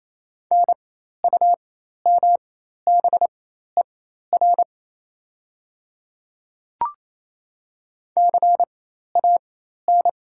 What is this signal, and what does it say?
Morse code, heard as a keyed tone, about 28 wpm, spaced as for 13 wpm, 700 Hz. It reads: NUMBER CAN
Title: 4x Character Spacing - Top 200 Words - 28wpm